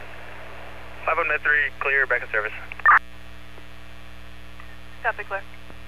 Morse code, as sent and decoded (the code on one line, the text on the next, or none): none